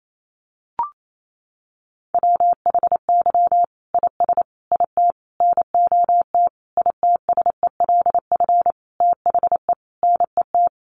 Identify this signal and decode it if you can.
Morse code, heard as a keyed tone, about 28 wpm, 700 Hz. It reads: WHY IS IT NOT ITSELF THE NET
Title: Sentences from Top 1400 Words 28wpm